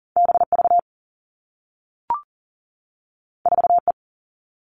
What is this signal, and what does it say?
Morse code, heard as a keyed tone, about 39 wpm, 700 Hz. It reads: BV 4E
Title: All Cut Numbers - 2-Digit 40wpm